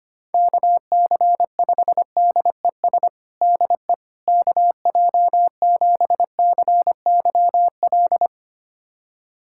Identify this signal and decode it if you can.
Morse code, heard as a keyed tone, about 25 wpm, 700 Hz. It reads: KC5DES DE KJ7CYL
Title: USA Callsign DE Callsign - 6 Characters - 25wpm